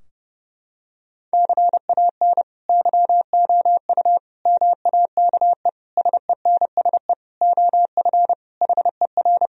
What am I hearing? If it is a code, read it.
Morse code, heard as a keyed tone, about 30 wpm, 700 Hz. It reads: CAN YOU MAKE SENSE OF HER
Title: Sentences from Top 1000 Words - Review Part 4 - 30wpm